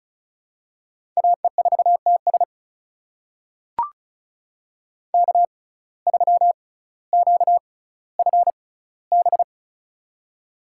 Morse code, heard as a keyed tone, about 36 wpm, spaced as for 18 wpm, 700 Hz. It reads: AE4TS K3QFB
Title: USA Call Signs - 5 Characters - 6x Character Spacing - 35wpm